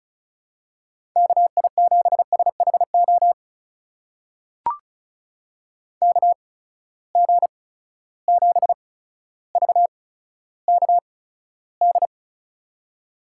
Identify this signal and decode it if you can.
Morse code, heard as a keyed tone, about 36 wpm, spaced as for 15 wpm, 700 Hz. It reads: KI7SHO KG7VKD
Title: USA Call Signs - 6 Characters - 8x Character Spacing - 35wpm